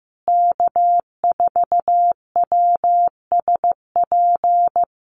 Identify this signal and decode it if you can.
Morse code, heard as a keyed tone, about 15 wpm, 700 Hz. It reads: K4WSP
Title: USA Call Signs Repeated 2x 15wpm